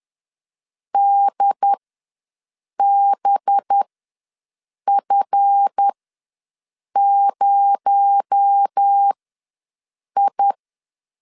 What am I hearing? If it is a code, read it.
Morse code, heard as a keyed tone, about 10 wpm, spaced as for 6 wpm, 800 Hz. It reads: DBF0I
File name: LEZIONI / 7 GF9Z / 7 GF9Z ws 10wpm 3919